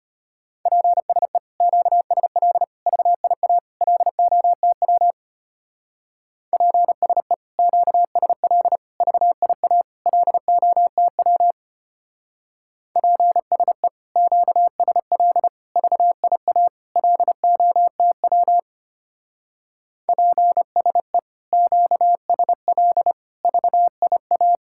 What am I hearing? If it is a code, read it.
Morse code, heard as a keyed tone, about 31 wpm, 700 Hz. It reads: PSE QSL VIA LOTW PSE QSL VIA LOTW PSE QSL VIA LOTW PSE QSL VIA
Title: Speed-Racing Common QSO Phrases - 25wpm